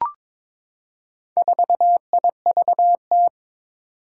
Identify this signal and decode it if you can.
Morse code, heard as a keyed tone, about 22 wpm, 700 Hz. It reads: 4IVT
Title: Four Challenging Letter-Number Combinations 22wpm